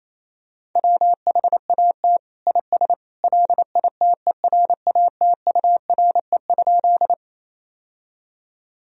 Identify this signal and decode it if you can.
Morse code, heard as a keyed tone, about 27 wpm, 700 Hz. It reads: WHAT IS LITERATURE?
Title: Sentences from Top 1000 Words - Review Part 3 - 28wpm